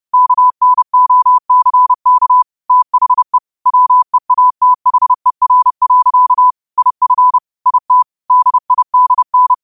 Morse code, heard as a keyed tone, about 30 wpm, 1000 Hz. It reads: KNOCK THE WEATHER. IF IT DIDN